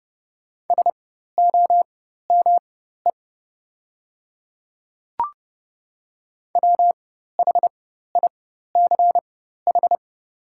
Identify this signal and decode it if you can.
Morse code, heard as a keyed tone, about 30 wpm, spaced as for 14 wpm, 690 Hz. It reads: SOME WHICH
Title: 4x Character Spacing - Top 500 Words - 30wpm